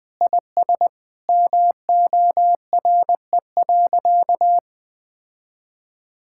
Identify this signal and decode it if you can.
Morse code, heard as a keyed tone, about 21 wpm, 700 Hz. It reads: IS MORE.
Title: Sentences with 3-Word Phrases - 20wpm